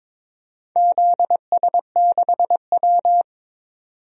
Morse code, heard as a keyed tone, about 22 wpm, 700 Hz. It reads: ZS6W
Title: DXCC-100 Call Signs - 4 Characters - 22wpm